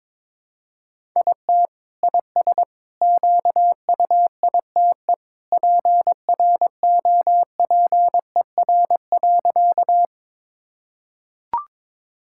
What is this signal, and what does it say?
Morse code, heard as a keyed tone, about 21 wpm, 700 Hz. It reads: IT IS QUITE PROPER.
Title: Sentences from Top 1000 Words - Review Part 5 - 22wpm